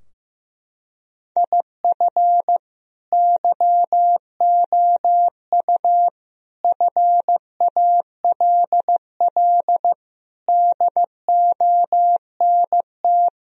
Morse code, heard as a keyed tone, about 16 wpm, 700 Hz. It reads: IF YOU FALL DONT
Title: Sentences from Top 1000 Words - Review Part 2 - 15wpm